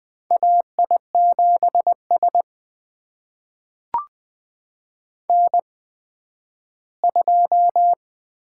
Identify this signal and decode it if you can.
Morse code, heard as a keyed tone, about 20 wpm, spaced as for 12 wpm, 700 Hz. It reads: AI7S N2
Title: USA Call Signs - 8x Character Spacing - 20wpm